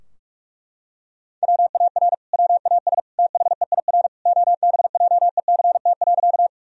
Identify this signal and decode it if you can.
Morse code, heard as a keyed tone, about 41 wpm, 700 Hz. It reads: WAR WAS THEIR OBJECT.
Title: Sentences from Top 1000 Words - Review Part 3 - 45wpm